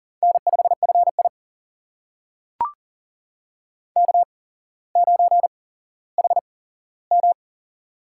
Morse code, heard as a keyed tone, about 40 wpm, spaced as for 17 wpm, 700 Hz. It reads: N5FI K9HM
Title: USA Call Signs - 4 Characters - 8x Character Spacing - 40wpm